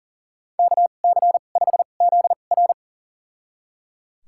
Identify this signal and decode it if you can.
Morse code, heard as a keyed tone, about 37 wpm, spaced as for 28 wpm, 700 Hz. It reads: KC5ZR
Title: USA Call Signs - 2x Character Spacing - 40wpm